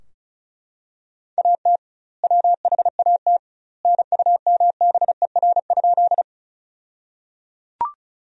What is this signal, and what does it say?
Morse code, heard as a keyed tone, about 36 wpm, 700 Hz. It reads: AT WHAT NUMBER?
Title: 2x Word-Spacing - Sentences from Top 100 Words - 35wpm